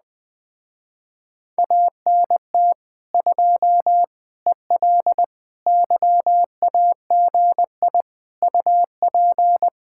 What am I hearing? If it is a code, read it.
Morse code, heard as a keyed tone, about 20 wpm, 700 Hz. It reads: ANT 2 EL YAGI UP